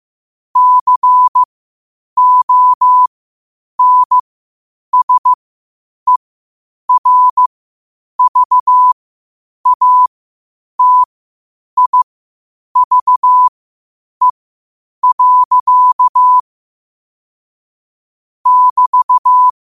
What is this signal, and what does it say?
Morse code, heard as a keyed tone, about 15 wpm, spaced as for 8 wpm, 1000 Hz. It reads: CONSERVATIVE. <BT>
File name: qotdfiles / 8wpm